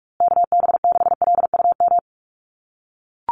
Morse code, heard as a keyed tone, about 44 wpm, 700 Hz. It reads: KB6LUM
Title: USA Call Signs Repeated 2x 45wpm